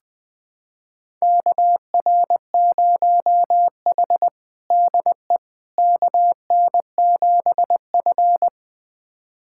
Morse code, heard as a keyed tone, about 20 wpm, 700 Hz. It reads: KR0H DE KN7F
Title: USA Callsign DE Callsign - 4 Characters - 20wpm